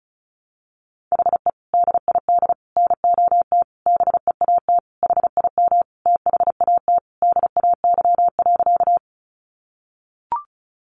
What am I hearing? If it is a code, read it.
Morse code, heard as a keyed tone, about 34 wpm, 700 Hz. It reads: HE DID NOT BEAT HIM THAT DAY.
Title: Sentences from Top 1300 Words 35wpm